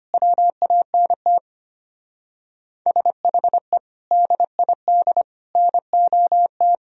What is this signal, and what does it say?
Morse code, heard as a keyed tone, about 31 wpm, 700 Hz. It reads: WANT SHE DID NOT
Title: Speed-Racing - Sets of 4 Words 25wpm